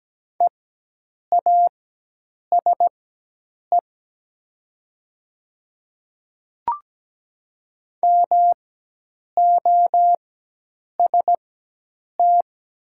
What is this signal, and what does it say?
Morse code, heard as a keyed tone, about 17 wpm, spaced as for 8 wpm, 700 Hz. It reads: EASE MOST
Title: 4x Character Spacing - Top 300 Words - 17wpm